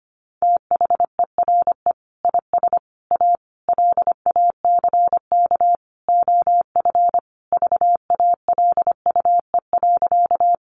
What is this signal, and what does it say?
Morse code, heard as a keyed tone, about 25 wpm, 700 Hz. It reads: THERE IS A LACK OF VALUE.
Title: Sentences from Top 1100 Words 25wpm